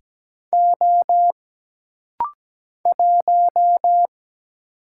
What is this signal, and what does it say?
Morse code, heard as a keyed tone, about 17 wpm, 700 Hz. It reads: O 1
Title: Single Letter-Number Alt - Rapid-Fire 17wpm